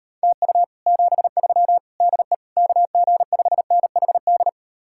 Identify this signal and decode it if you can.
Morse code, heard as a keyed tone, about 38 wpm, 700 Hz. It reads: TU 73 DE KG5NHD